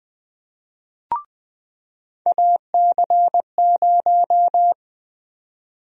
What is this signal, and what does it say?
Morse code, heard as a keyed tone, about 20 wpm, 700 Hz. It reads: AC0
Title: USA Call Sign Prefixes - 20wpm